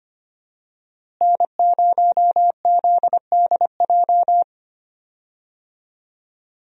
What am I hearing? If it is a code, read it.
Morse code, heard as a keyed tone, about 25 wpm, 700 Hz. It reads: N0ZDJ
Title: USA Call Signs - 5 Characters - Repeated 2x - 25wpm